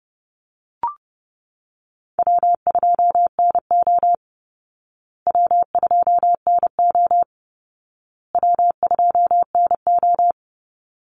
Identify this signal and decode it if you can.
Morse code, heard as a keyed tone, about 30 wpm, 700 Hz. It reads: W2NO W2NO W2NO
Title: Easy Call Signs Repeated 3x - 30wpm